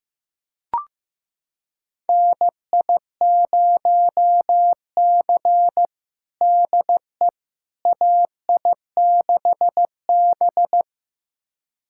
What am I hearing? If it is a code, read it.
Morse code, heard as a keyed tone, about 16 wpm, 700 Hz. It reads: NI0C DE AI6B
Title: USA Callsign DE Callsign - 4 Characters - 15wpm